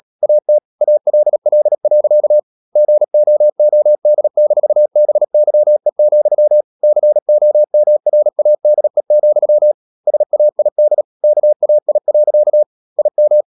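Morse code, heard as a keyed tone, about 37 wpm, 600 Hz. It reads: AT ALL. GOOD-BYE, COMRADE, SAID KAI. IM
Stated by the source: Morse code practice at 35–39 wpm sending English text of Ken Follett's novel Never, from CW med Gnister.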